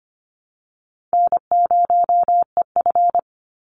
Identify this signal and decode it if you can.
Morse code, heard as a keyed tone, about 25 wpm, 700 Hz. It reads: N0EF
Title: Easy Call Signs Repeated 3x - 25wpm